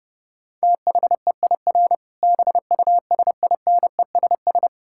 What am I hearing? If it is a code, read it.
Morse code, heard as a keyed tone, about 32 wpm, 700 Hz. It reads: THEIR BUSINESS
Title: Sentences from Top 1000 Words - Review Part 5 - 30wpm